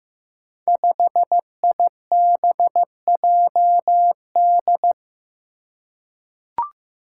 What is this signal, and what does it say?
Morse code, heard as a keyed tone, about 15 wpm, 700 Hz. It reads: 5IBJD
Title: Five Challenging Letter-Number Combinations 15wpm